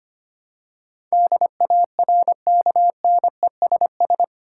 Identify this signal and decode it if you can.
Morse code, heard as a keyed tone, about 25 wpm, 700 Hz. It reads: DARKNESS
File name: Eight Letter Words - 25wpm